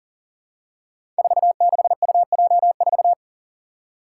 Morse code, heard as a keyed tone, about 40 wpm, 700 Hz. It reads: 46UJ4
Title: Five Challenging Letter-Number Combinations 40wpm